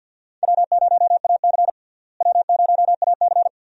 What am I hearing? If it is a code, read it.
Morse code, heard as a keyed tone, about 50 wpm, 700 Hz. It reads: W0AC W0AC
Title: USA Call Signs - 4 Characters - Repeated 2x - 50wpm